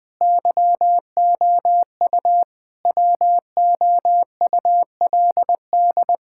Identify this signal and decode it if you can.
Morse code, heard as a keyed tone, about 20 wpm, 700 Hz. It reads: YOU WOULD
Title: Sets of 9 Words - 20wpm